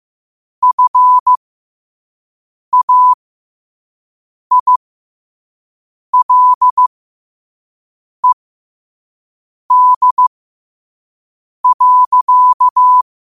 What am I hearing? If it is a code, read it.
Morse code, heard as a keyed tone, about 15 wpm, spaced as for 5 wpm, 1000 Hz. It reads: FAILED.
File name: 5wpm